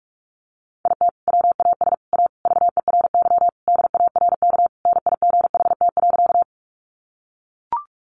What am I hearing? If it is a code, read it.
Morse code, heard as a keyed tone, about 44 wpm, 700 Hz. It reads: IT WAS A VERY DARK NIGHT.
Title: Sentences from Top 500 Words 45wpm